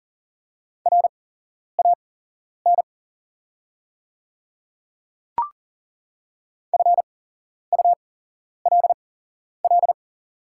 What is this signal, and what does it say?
Morse code, heard as a keyed tone, about 40 wpm, spaced as for 11 wpm, 690 Hz. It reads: RAN FULL
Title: 8x Character Spacing - Top 500 Words - 40wpm